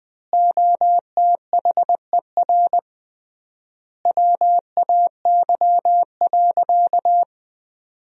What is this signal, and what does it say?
Morse code, heard as a keyed tone, about 20 wpm, 700 Hz. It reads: OTHER WAY.
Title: 3x Word-Spacing - Sentences from Top 100 Words - 20wpm